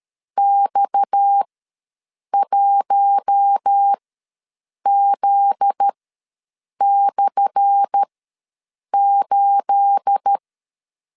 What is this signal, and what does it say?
Morse code, heard as a keyed tone, about 13 wpm, spaced as for 7 wpm, 800 Hz. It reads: X1Z/8
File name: LEZIONI / 10 SIGNS / 10A SLASH ws 12wpm 3443